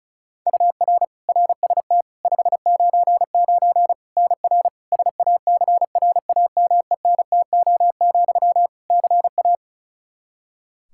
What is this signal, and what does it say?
Morse code, heard as a keyed tone, about 36 wpm, 700 Hz. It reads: UR RST 599 NR SACRAMENTO, CA
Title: Practice QSO - Exchange 1 - Beginning - 35wpm